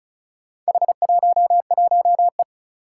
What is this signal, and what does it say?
Morse code, heard as a keyed tone, about 36 wpm, 700 Hz. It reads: H11E
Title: Four Challenging Letter-Number Combinations 35wpm